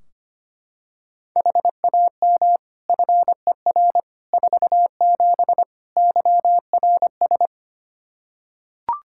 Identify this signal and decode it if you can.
Morse code, heard as a keyed tone, about 25 wpm, 700 Hz. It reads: HAM FER 47 YRS
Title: K1USN SST QSO Practice 25wpm